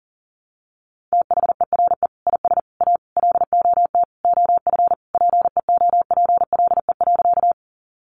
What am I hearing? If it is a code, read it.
Morse code, heard as a keyed tone, about 40 wpm, 700 Hz. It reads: THERE IS A LOT OF PEOPLE.
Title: Sentences from Top 500 Words 40wpm